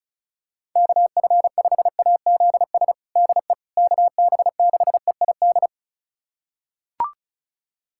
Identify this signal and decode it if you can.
Morse code, heard as a keyed tone, about 34 wpm, 700 Hz. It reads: KF5AZS DE KB6EID
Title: USA Callsign DE Callsign - 6 Characters - 35wpm